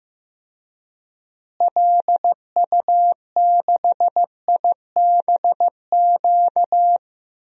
Morse code, heard as a keyed tone, about 15 wpm, 700 Hz. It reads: LU6IBQ
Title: DXCC-100 Call Signs - 6 Characters - Repeated 2x - 15wpm